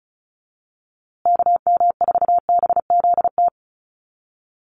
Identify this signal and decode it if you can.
Morse code, heard as a keyed tone, about 34 wpm, 700 Hz. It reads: KM4BZT
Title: USA Call Signs Repeated 2x 35wpm